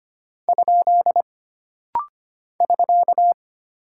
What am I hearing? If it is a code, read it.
Morse code, heard as a keyed tone, about 25 wpm, 700 Hz. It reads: ? <SK>
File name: Single Punctuation-Prosign - Rapid-Fire - 25wpm